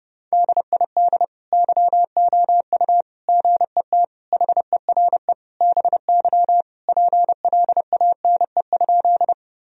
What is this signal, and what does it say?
Morse code, heard as a keyed tone, about 30 wpm, 700 Hz. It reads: DID YOU GET HERE BY PLANE?
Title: Sentences from Top 1000 Words - Review Part 3 - 30wpm